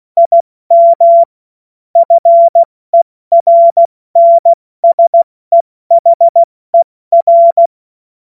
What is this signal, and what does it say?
Morse code, heard as a keyed tone, about 16 wpm, 680 Hz. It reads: IM FERNSEHER
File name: Morsekurs / Witze / Witz 24. August 2021, 70 F80